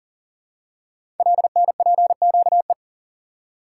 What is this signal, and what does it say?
Morse code, heard as a keyed tone, about 39 wpm, 690 Hz. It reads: LNPQE
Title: Code Groups - 5 Letters - 40wpm